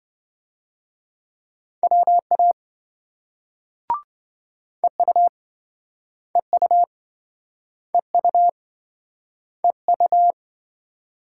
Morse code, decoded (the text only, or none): WA EU EU EU EU